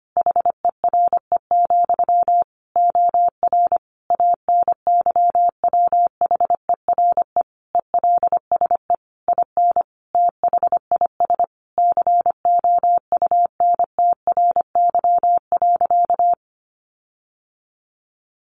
Morse code, heard as a keyed tone, about 25 wpm, 700 Hz. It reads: HERE, OR ANYWHERE ELSE IN THIS COUNTRY.
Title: Sentences from Top 1400 Words 25wpm